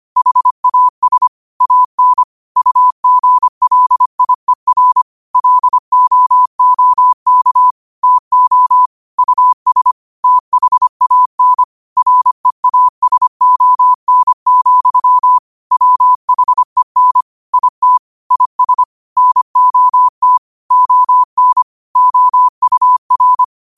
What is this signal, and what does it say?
Morse code, heard as a keyed tone, about 25 wpm, 1000 Hz. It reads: HAS AN UGLIER LOOK TO US THAN REASON, WHEN IT IS NOT ON OUR
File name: qotdfiles / 25wpm